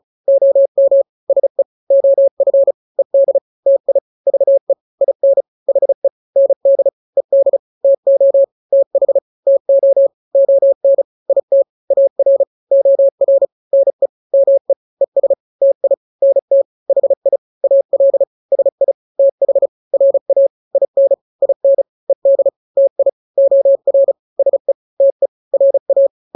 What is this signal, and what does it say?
Morse code, heard as a keyed tone, about 34 wpm, 550 Hz. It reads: OM SE OF ED TI VE IN HE ND ED TO TH TO ON IT AR OR NE ME ES TI NT HI AL SI TH RA IN IN ED TI OR SE TE RA
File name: n-grams / 2-grams 35 30 WPM